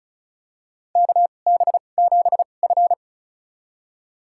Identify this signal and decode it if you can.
Morse code, heard as a keyed tone, about 36 wpm, spaced as for 25 wpm, 700 Hz. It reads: KB7F
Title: USA Call Signs - 2x Character Spacing - 35wpm